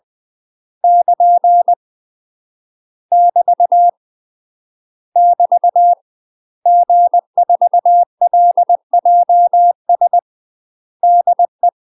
Morse code, heard as a keyed tone, about 20 wpm, 700 Hz. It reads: <KN> <BT> <BT> G4LJS DE
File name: UK MASTER QSOs TRAINING / 20WPM / UK MASTER QSOs 0038